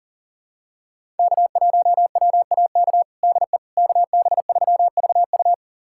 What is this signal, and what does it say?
Morse code, heard as a keyed tone, about 40 wpm, 700 Hz. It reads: K1WAK DE KB3VU
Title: USA Callsign DE Callsign - 5 Characters - 40wpm